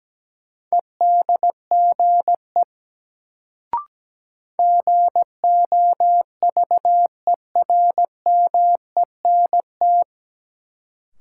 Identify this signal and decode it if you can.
Morse code, heard as a keyed tone, about 17 wpm, 700 Hz. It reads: EDGE GOVERMENT
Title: Sentences from Top 1300 Words - New Word Drill Alt - 17wpm